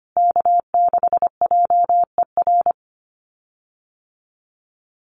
Morse code, heard as a keyed tone, about 25 wpm, 700 Hz. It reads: K6JER
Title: USA Call Signs Repeated 3x 25wpm